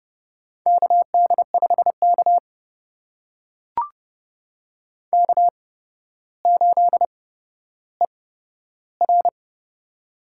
Morse code, heard as a keyed tone, about 31 wpm, spaced as for 13 wpm, 700 Hz. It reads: KD5K K8ER
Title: USA Call Signs - 4 Characters - 8x Character Spacing - 30wpm